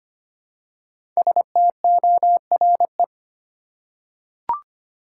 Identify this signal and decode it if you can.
Morse code, heard as a keyed tone, about 25 wpm, 700 Hz. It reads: STORE
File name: 5 Letter Words - First 3 Letters Given - 25wpm